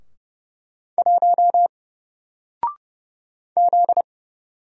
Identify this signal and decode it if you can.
Morse code, heard as a keyed tone, about 30 wpm, 700 Hz. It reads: 1 Z
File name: Single-Letter-Number-30wpm